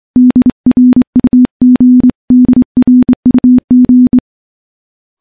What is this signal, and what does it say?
Morse code, heard as a keyed tone, about 23 wpm, 250 Hz. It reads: DRUG DRUG
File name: morse / morse-training-25WPM